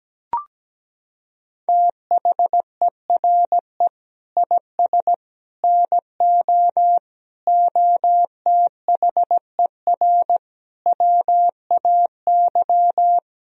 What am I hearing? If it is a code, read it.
Morse code, heard as a keyed tone, about 17 wpm, 700 Hz. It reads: THERE IS NO OTHER WAY
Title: Sentences from Top 1000 Words - Review Part 2 - 17wpm